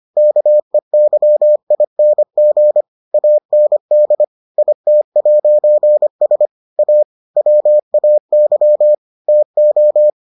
Morse code, heard as a keyed tone, about 25 wpm, 600 Hz. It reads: KEYING AND IT'S A WAY TO